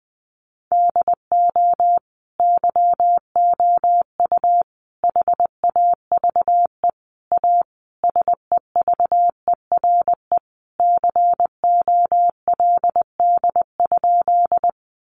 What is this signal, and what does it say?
Morse code, heard as a keyed tone, about 19 wpm, 700 Hz. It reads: DO YOU HAVE A SEVERE COLD?
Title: Sentences from Top 1400 Words 20wpm